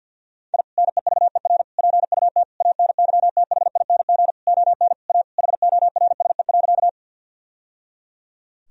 Morse code, heard as a keyed tone, about 49 wpm, 700 Hz. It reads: I NEVER PUT ANYTHING ON A HORSE.
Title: Sentences from Top 1000 Words - Review Part 1 - 50wpm